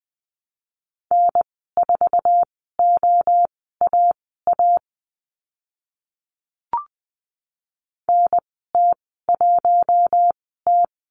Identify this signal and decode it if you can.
Morse code, heard as a keyed tone, about 20 wpm, spaced as for 15 wpm, 700 Hz. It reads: N4OAA NT1T
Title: 048 Lesson 9g - Number 4 - Call Signs 20wpm